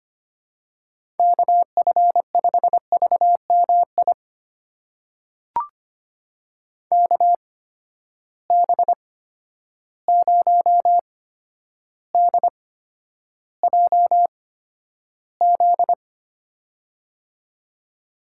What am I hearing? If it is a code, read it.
Morse code, heard as a keyed tone, about 25 wpm, spaced as for 11 wpm, 700 Hz. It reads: KF5VMI KB0DJZ
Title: USA Call Signs - 6 Characters - 8x Character Spacing - 25wpm